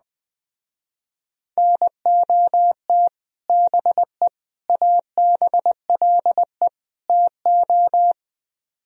Morse code, heard as a keyed tone, about 21 wpm, 700 Hz. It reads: NOT BE ABLE TO